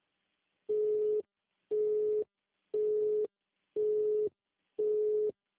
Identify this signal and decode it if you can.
Busy tone (Europe, 425 Hz).
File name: out- -2001- - - .33292